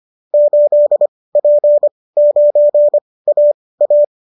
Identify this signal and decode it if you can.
Morse code, heard as a keyed tone, about 25 wpm, spaced as for 18 wpm, 590 Hz. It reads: 8P9AA